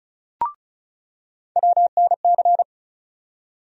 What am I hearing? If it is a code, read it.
Morse code, heard as a keyed tone, about 36 wpm, 700 Hz. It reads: WNC
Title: USA Call Sign Postfixes - 3 Letters 35wpm